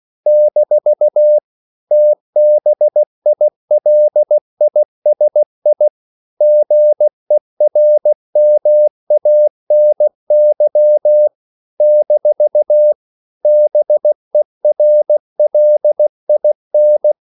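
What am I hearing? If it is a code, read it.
Morse code, heard as a keyed tone, about 16 wpm, 600 Hz. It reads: - TBILISI GERMANY - BERLIN
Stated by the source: CW practice of national capitals at 16 words per minute, from ON6ZQ.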